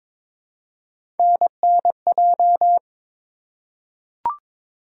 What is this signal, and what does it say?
Morse code, heard as a keyed tone, about 22 wpm, 700 Hz. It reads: NNJ
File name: practice / 22wpm / ARRL Sections v2 - 22wpm